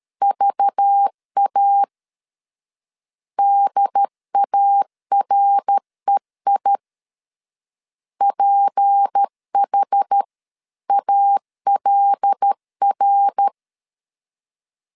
Morse code, heard as a keyed tone, about 13 wpm, 800 Hz. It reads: VA DAREI PH ALR
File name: LEZIONI / 6 PKX0 / 6 P 12wpm 2033